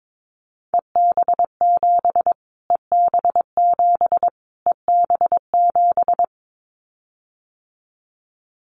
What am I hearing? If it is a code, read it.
Morse code, heard as a keyed tone, about 22 wpm, 700 Hz. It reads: EB7 EB7 EB7
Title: Callsign Prefixes for Top 25 DXCC - Repeated 3x - 22wpm